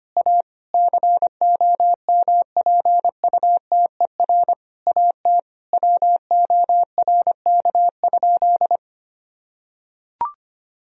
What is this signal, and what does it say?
Morse code, heard as a keyed tone, about 25 wpm, 700 Hz. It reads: A COMPUTER AT WORK?
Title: Sentences from Top 1000 Words - Review Part 3 - 25wpm